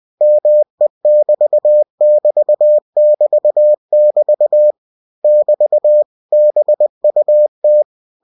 Morse code, heard as a keyed tone, about 20 wpm, 600 Hz. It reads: ME<BT><BT><BT><BT> <BT> BUT
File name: cw-books / winnie-the-pooh / winnie-20wpm-page-0087